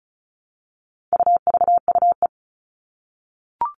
Morse code, heard as a keyed tone, about 34 wpm, 700 Hz. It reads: UVUE
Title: All Cut Numbers - 4-Digit 35wpm